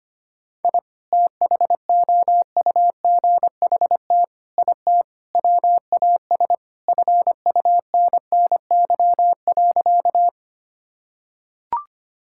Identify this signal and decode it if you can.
Morse code, heard as a keyed tone, about 25 wpm, 700 Hz. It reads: I THOUGHT IT WAS FUNNY.
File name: practice / 25wpm / Sentences with 4-Word Phrases - 25wpm